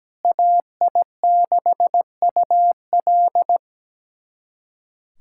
Morse code, heard as a keyed tone, about 17 wpm, 700 Hz. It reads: AI6UL
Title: USA Callsign DE Callsign - 5 Characters - 17wpm